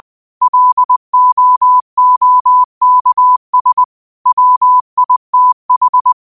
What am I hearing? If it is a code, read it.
Morse code, heard as a keyed tone, about 20 wpm, 1000 Hz. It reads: LOOKS WITH
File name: qotdfiles / 20wpm